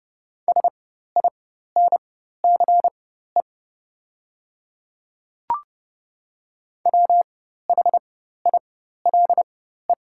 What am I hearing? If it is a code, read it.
Morse code, heard as a keyed tone, about 30 wpm, spaced as for 14 wpm, 690 Hz. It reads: SINCE WHILE
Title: 4x Character Spacing - Top 500 Words - 30wpm